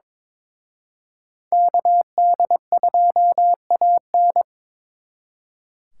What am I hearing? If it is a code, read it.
Morse code, heard as a keyed tone, about 26 wpm, 700 Hz. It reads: KD2AN
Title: Speed Racing - USA 2x2 Call Signs 22wpm